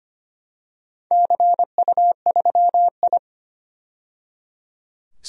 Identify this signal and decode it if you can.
Morse code, heard as a keyed tone, about 25 wpm, 700 Hz. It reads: CU3I
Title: DXCC-100 Call Signs - 4 Characters - 25wpm